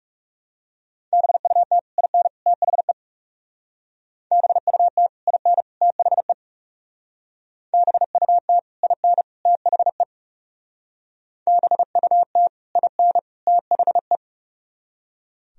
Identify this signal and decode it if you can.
Morse code, heard as a keyed tone, about 39 wpm, 700 Hz. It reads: BUT IN THE BUT IN THE BUT IN THE BUT IN THE